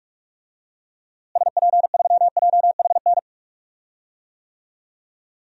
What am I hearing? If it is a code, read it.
Morse code, heard as a keyed tone, about 41 wpm, 700 Hz. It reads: SP3JHN
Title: DXCC-100 Call Signs - 6 Characters - Repeated 2x - 45wpm